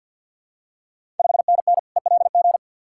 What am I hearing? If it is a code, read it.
Morse code, heard as a keyed tone, about 51 wpm, 690 Hz. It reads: 5NN ELG
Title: Ontario QSO Party Exchange 50wpm